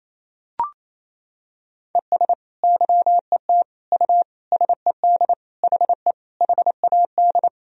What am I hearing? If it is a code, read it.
Morse code, heard as a keyed tone, about 28 wpm, 700 Hz. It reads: ES YET U SED HE HAD
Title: Sentences from Top 500 Words and Abbreviations 28wpm